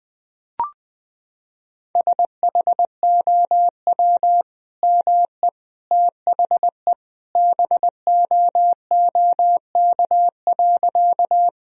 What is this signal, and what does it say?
Morse code, heard as a keyed tone, about 20 wpm, 700 Hz. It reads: SHOW ME THE BOOK.
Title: Sentences from Top 1000 Words - Review Part 1 - 20wpm